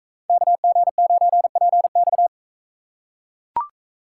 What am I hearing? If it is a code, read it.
Morse code, heard as a keyed tone, about 39 wpm, 700 Hz. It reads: KG9PX